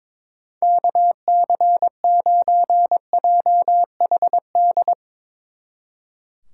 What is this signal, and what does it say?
Morse code, heard as a keyed tone, about 24 wpm, 700 Hz. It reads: KC9JHD
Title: Speed Racing - USA 2x3 Call Signs 22wpm